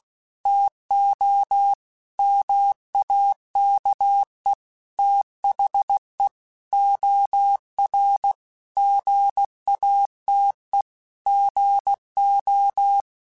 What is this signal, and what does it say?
Morse code, heard as a keyed tone, about 16 wpm, 800 Hz. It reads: TO MAKE THE OR GATE GO
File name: Technical Subjects / Technical Subjects-mp3s / OR Gate @15